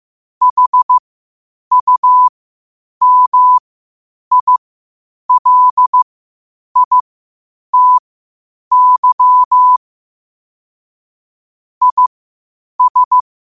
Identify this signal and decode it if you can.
Morse code, heard as a keyed tone, about 15 wpm, spaced as for 8 wpm, 1000 Hz. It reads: HUMILITY IS
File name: qotdfiles / 8wpm